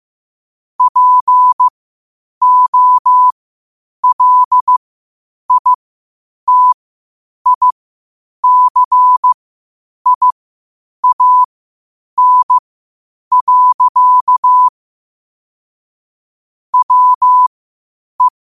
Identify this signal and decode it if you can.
Morse code, heard as a keyed tone, about 15 wpm, spaced as for 8 wpm, 1000 Hz. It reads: POLITICIAN. WE
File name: qotdfiles / 8wpm